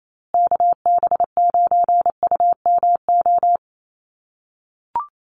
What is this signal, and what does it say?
Morse code, heard as a keyed tone, about 28 wpm, 700 Hz. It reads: KB9UMO
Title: USA Call Signs Repeated 2x 28wpm